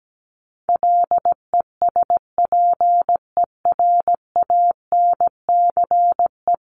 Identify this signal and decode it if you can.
Morse code, heard as a keyed tone, about 17 wpm, 700 Hz. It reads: LESPERANCE
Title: A Sound Of Thunder 17wpm